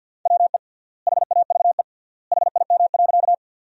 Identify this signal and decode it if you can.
Morse code, heard as a keyed tone, about 50 wpm, 700 Hz. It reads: WE HAVE HIM.
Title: 3x Word-Spacing - Sentences from Top 100 Words - 50wpm